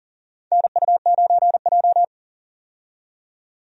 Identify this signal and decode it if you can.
Morse code, heard as a keyed tone, about 37 wpm, 700 Hz. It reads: NU9J
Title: USA Call Signs - 4 Characters - 40wpm